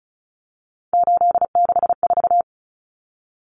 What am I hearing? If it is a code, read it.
Morse code, heard as a keyed tone, about 34 wpm, 700 Hz. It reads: 864
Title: Common Cut Numbers - 3-Digit 35wpm